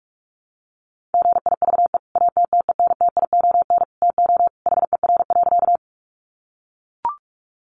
Morse code, heard as a keyed tone, about 44 wpm, 700 Hz. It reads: GIVE ATTENTION TO HER.
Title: Sentences from Top 1000 Words 45wpm